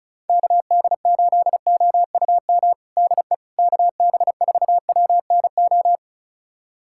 Morse code, heard as a keyed tone, about 36 wpm, 700 Hz. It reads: KD8OUM DE KB4WNO
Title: USA Callsign DE Callsign 35wpm